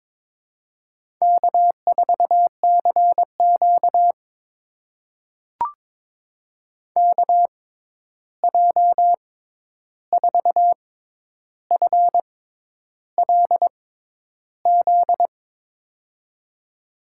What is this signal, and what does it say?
Morse code, heard as a keyed tone, about 22 wpm, spaced as for 10 wpm, 700 Hz. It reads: K4CQ KJ4FLZ
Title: USA Call Signs - 6x Character Spacing - 22wpm